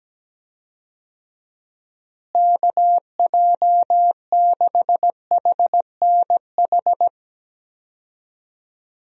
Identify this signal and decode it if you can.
Morse code, heard as a keyed tone, about 17 wpm, 700 Hz. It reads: KJ6HNH
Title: USA Call Signs - 6 Characters - Repeated 3x - 17wpm